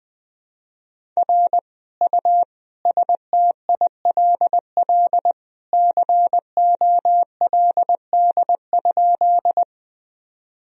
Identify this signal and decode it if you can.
Morse code, heard as a keyed tone, about 20 wpm, 700 Hz. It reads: R U STILL COLD?
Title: Sentences from Top 400 Words and Abbreviations 20wpm